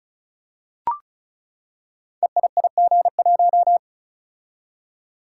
Morse code, heard as a keyed tone, about 34 wpm, 700 Hz. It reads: EIIG1
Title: Code Groups - 5 Letter:Numbers - 35wpm